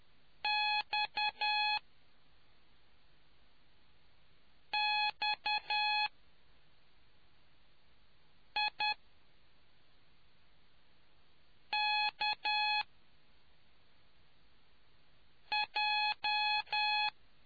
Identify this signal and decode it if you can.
Morse code, heard as a keyed tone, about 10 wpm, spaced as for 3 wpm, 840 Hz. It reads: XXIKJ